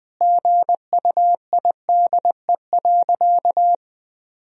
Morse code, heard as a keyed tone, about 20 wpm, 700 Hz. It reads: GUIDE.
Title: Sentences from Top 1000 Words - Review Part 3 - 20wpm